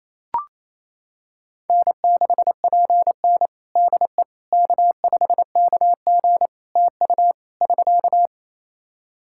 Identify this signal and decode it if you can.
Morse code, heard as a keyed tone, about 27 wpm, 700 Hz. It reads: N6PN DE K5KG TU <SK>
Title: Practice QSO - Exchange 4 - Ending - 28wpm